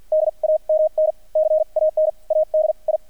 Morse code, heard as a keyed tone, about 48 wpm, spaced as for 37 wpm, 620 Hz. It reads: DAMN CANADI
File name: archived / CRSnet